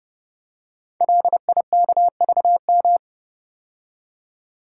Morse code, heard as a keyed tone, about 30 wpm, 700 Hz. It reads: LIKVM
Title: Code Groups - 5 Letters - 30wpm